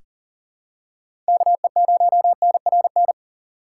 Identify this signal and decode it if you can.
Morse code, heard as a keyed tone, about 39 wpm, 700 Hz. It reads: KE0NRN
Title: USA Call Signs - 6x Character Spacing - 40wpm